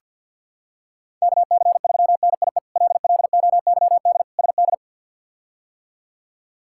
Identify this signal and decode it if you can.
Morse code, heard as a keyed tone, about 50 wpm, 690 Hz. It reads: KC3NIE LLOYD SD
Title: K1USN SST QSO Practice - 50wpm